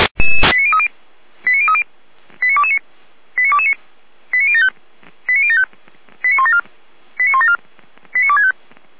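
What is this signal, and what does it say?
Five-tone selective call (ZVEI), heard as quick runs of five tones, tones of about 70 ms. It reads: 89030 89030 89229 89229 89075 89075 89174 89174 89265
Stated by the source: ZVEI five-tone selective call 89067 received with an SDR, from the Feuerwehr Orlamünde audio server.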